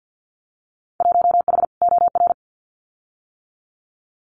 Morse code, heard as a keyed tone, about 50 wpm, 700 Hz. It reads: 1H OR
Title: Winter Field Day Exchanges 50wpm